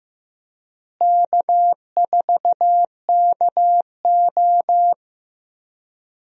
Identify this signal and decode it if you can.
Morse code, heard as a keyed tone, about 15 wpm, 700 Hz. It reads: K4KO
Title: USA Call Signs - 4 Characters - Repeated 2x - 15wpm